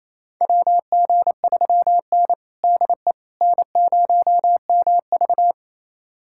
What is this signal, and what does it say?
Morse code, heard as a keyed tone, about 28 wpm, 700 Hz. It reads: WG3N DE N0MV
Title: USA Callsign DE Callsign - 4 Characters - 28wpm